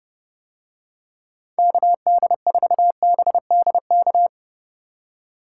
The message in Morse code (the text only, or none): KD4BDK